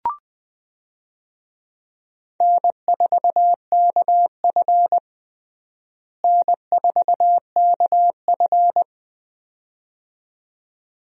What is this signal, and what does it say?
Morse code, heard as a keyed tone, about 20 wpm, 700 Hz. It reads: N4KF N4KF
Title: USA Call Signs - 4 Characters - Repeated 2x - 20wpm